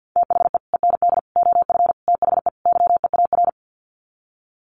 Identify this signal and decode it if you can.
Morse code, heard as a keyed tone, about 48 wpm, 700 Hz. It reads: THE END OF THE YEAR
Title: Sets of 5 Words 50wpm